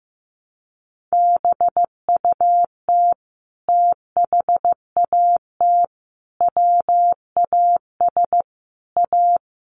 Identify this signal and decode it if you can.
Morse code, heard as a keyed tone, about 15 wpm, 700 Hz. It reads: BUT THAT WAS A
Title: Sentences from Top 1100 Words 15wpm